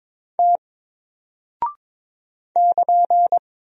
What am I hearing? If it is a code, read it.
Morse code, heard as a keyed tone, about 22 wpm, 700 Hz. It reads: T <KN>
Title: Single Character (Letter:Number:Punctuation:Prosign) 22wpm